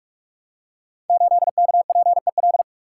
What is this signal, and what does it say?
Morse code, heard as a keyed tone, about 44 wpm, 700 Hz. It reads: 8KPEL
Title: Code Groups - 5 Letter:Numbers - 45wpm